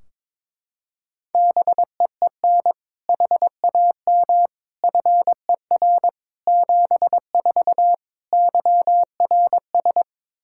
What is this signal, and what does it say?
Morse code, heard as a keyed tone, about 22 wpm, 700 Hz. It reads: BEEN HAM FER 74 YRS
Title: K1USN SST QSO Practice 22wpm